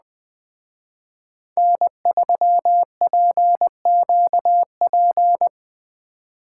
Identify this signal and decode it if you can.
Morse code, heard as a keyed tone, about 20 wpm, 700 Hz. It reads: N3PQP